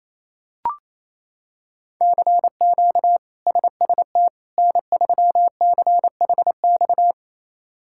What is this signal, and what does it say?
Morse code, heard as a keyed tone, about 28 wpm, 700 Hz. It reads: CQ SST N3CHX
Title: K1USN SST QSO Practice - 28wpm